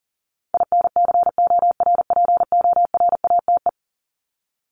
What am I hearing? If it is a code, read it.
Morse code, heard as a keyed tone, about 40 wpm, 700 Hz. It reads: INCORPORATE
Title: Top 1751-2000 Words 40wpm